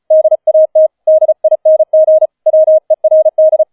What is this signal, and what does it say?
Morse code, heard as a keyed tone, about 35 wpm, 630 Hz. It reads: DAT DING WERD